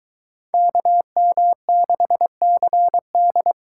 Morse code, heard as a keyed tone, about 23 wpm, 700 Hz. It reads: KM6CD